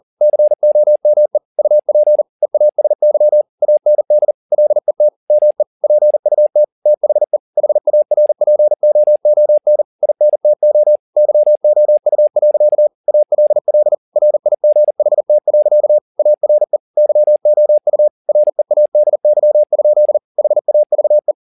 Morse code, heard as a keyed tone, about 40 wpm, 600 Hz. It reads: COME UP EASY AND LET ME PUT THE HARPOON INTO YOU. ALL RIGHT. ARE YOU READY? HAVE